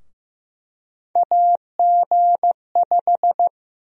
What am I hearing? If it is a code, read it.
Morse code, heard as a keyed tone, about 15 wpm, 700 Hz. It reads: AG5
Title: USA Call Sign Prefixes - 15wpm